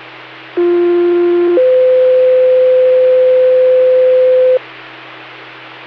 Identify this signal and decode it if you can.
Two-tone page, heard as a one-second tone, then a three-second tone, about 350 then 510 Hz. It is heard from 0.6 to 4.6 s.